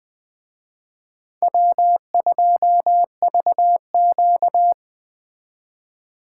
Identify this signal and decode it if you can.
Morse code, heard as a keyed tone, about 20 wpm, 700 Hz. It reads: W2VQ